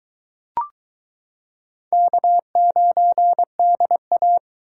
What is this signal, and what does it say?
Morse code, heard as a keyed tone, about 22 wpm, 700 Hz. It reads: K9DA